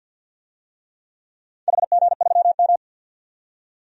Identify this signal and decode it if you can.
Morse code, heard as a keyed tone, about 51 wpm, 700 Hz. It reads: HG3M
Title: DXCC-100 Call Signs - 4 Characters - 50wpm